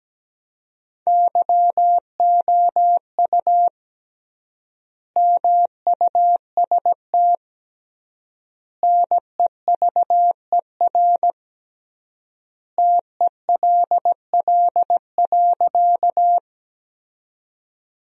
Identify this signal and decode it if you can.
Morse code, heard as a keyed tone, about 17 wpm, 700 Hz. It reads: YOU MUST NEVER TELL.
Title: 3x Word-Spacing - Sentences from Top 300 Words - 17wpm